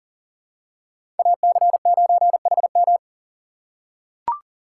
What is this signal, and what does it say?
Morse code, heard as a keyed tone, about 40 wpm, 700 Hz. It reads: AC9HM
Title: USA Call Signs - 5 Characters - 2x Character Spacing - 40wpm